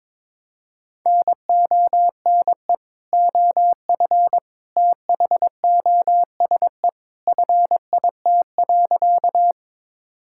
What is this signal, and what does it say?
Morse code, heard as a keyed tone, about 22 wpm, 700 Hz. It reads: NONE OF THOSE FIT.
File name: practice / 22wpm / Sentences with 2-Word Phrases - 22wpm